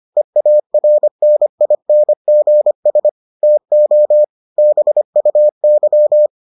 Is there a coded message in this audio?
Yes: Morse code from 0.2 to 6.3 s, heard as a keyed tone, about 25 wpm, 600 Hz. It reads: EARNINGS TO BUY